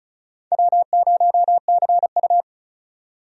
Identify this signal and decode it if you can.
Morse code, heard as a keyed tone, about 36 wpm, 690 Hz. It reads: W0CU
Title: USA Call Signs - 4 Characters - 35wpm